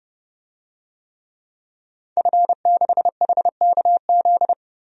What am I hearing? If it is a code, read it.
Morse code, heard as a keyed tone, about 32 wpm, 700 Hz. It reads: F6HKZ
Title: DXCC-100 Call Signs - 5 Characters - Repeated 2x - 30wpm